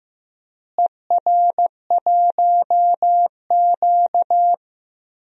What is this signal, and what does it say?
Morse code, heard as a keyed tone, about 15 wpm, 700 Hz. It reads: ER1Q
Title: DXCC-100 Call Signs - 4 Characters - Repeated 2x - 15wpm